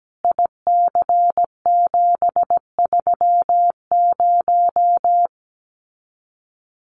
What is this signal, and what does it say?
Morse code, heard as a keyed tone, about 17 wpm, 700 Hz. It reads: IC730
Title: Common QSO Phrases 17wpm